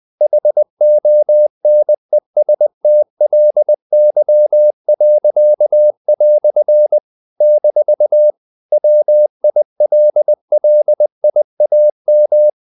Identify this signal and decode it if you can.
Morse code, heard as a keyed tone, about 20 wpm, 600 Hz. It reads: HONESTLY." - WILLIAM
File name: CW / Practice / EnglishQuotations0320